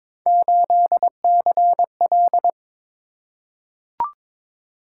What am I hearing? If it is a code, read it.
Morse code, heard as a keyed tone, about 22 wpm, 700 Hz. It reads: 8CL
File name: practice / 22wpm / Three Letter-Number Combinations - 22wpm